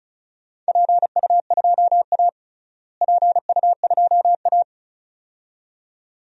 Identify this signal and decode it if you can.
Morse code, heard as a keyed tone, about 34 wpm, 700 Hz. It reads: PU2A PU2A
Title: DXCC-100 Call Signs - 4 Characters - Repeated 2x - 35wpm